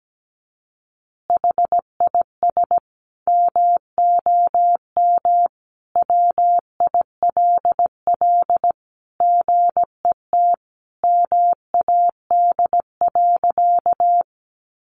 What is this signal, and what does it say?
Morse code, heard as a keyed tone, about 17 wpm, 700 Hz. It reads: HIS MOM WILL GET MAD.
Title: Additional Easy Phrases 17wpm